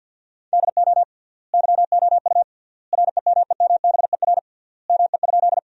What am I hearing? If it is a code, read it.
Morse code, heard as a keyed tone, about 50 wpm, 700 Hz. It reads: DO YOU REMEMBER ME?
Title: 3x Word-Spacing - Sentences from Top 400 Words - 50wpm